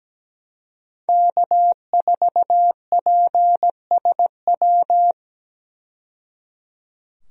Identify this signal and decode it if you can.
Morse code, heard as a keyed tone, about 17 wpm, 700 Hz. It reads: K4PSW
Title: USA Call Signs - 5 Characters - 17wpm